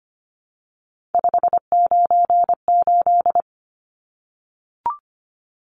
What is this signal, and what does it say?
Morse code, heard as a keyed tone, about 25 wpm, 700 Hz. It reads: 598
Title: Numbers - 3-Digit 25wpm